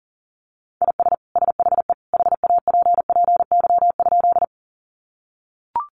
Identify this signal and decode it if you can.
Morse code, heard as a keyed tone, about 40 wpm, 700 Hz. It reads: IS SHE HAPPY?
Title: Sentences from Top 1000 Words 40wpm